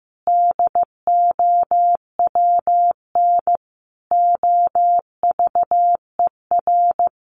Morse code, heard as a keyed tone, about 15 wpm, 700 Hz. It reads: DOWN OVER
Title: Sentences from Top 800 Words 15wpm